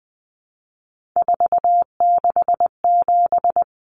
Morse code, heard as a keyed tone, about 20 wpm, 700 Hz. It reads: 467
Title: Common Cut Numbers - 3-Digit 20wpm